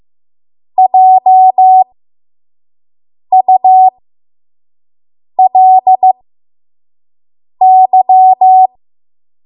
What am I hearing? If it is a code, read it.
Morse code, heard as a keyed tone, about 15 wpm, spaced as for 5 wpm, 750 Hz. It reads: JULY
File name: Morse / 05WPM